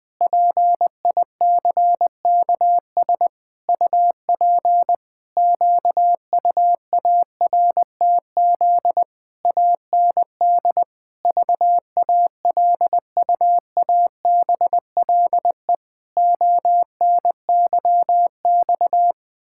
Morse code, heard as a keyed tone, about 20 wpm, 700 Hz. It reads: PICKS UP QUARTZ AND VALUABLE ONYX
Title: Pangram Challenge v2 - 20wpm